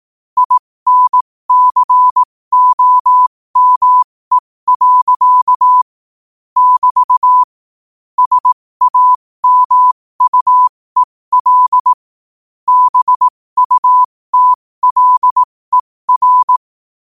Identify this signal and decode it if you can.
Morse code, heard as a keyed tone, about 18 wpm, spaced as for 15 wpm, 1000 Hz. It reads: INCOME. <BT> SAMUEL BUTLER
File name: qotdfiles / 15wpm